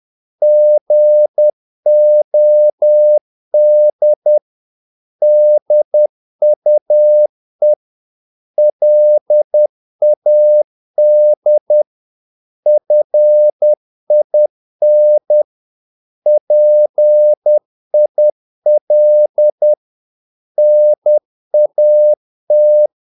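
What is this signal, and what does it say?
Morse code, heard as a keyed tone, about 10 wpm, 600 Hz. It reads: GOD DUE LAD FIN PIL NAT
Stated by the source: Morse practice of short Danish words at 10 wpm, from CW med Gnister.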